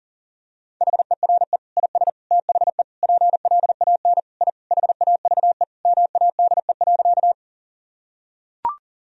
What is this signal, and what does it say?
Morse code, heard as a keyed tone, about 40 wpm, 700 Hz. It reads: HERE IS THE PLAN I HAVE MADE.
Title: Sentences from Top 1000 Words - Review Part 2 - 40wpm